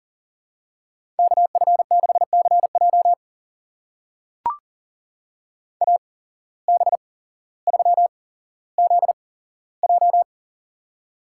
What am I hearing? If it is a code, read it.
Morse code, heard as a keyed tone, about 38 wpm, spaced as for 17 wpm, 700 Hz. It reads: KF6CJ AB3ZJ
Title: USA Call Signs - 5 Characters - 8x Character Spacing - 40wpm